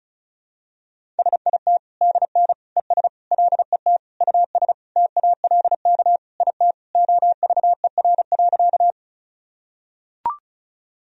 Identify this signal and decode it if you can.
Morse code, heard as a keyed tone, about 36 wpm, 700 Hz. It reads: SIT DN ES LET US TALK IT OVER.
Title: Sentences from Top 500 Words and Abbreviations 35wpm